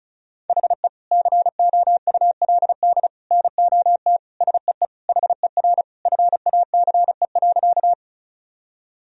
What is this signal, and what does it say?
Morse code, heard as a keyed tone, about 36 wpm, 700 Hz. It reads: HE COULD NOT SEE HER FACE.
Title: Sentences from Top 1000 Words - Review Part 2 - 35wpm